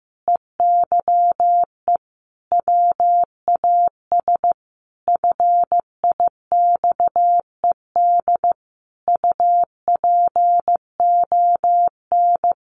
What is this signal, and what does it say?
Morse code, heard as a keyed tone, about 15 wpm, 700 Hz. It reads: EYE WAS FIXED UPON
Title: Sentences from Top 1100 Words 15wpm